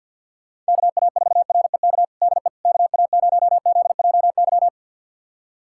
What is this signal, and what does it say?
Morse code, heard as a keyed tone, about 45 wpm, 700 Hz. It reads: KA4REK DE KA0ZJY
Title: USA Callsign DE Callsign 50wpm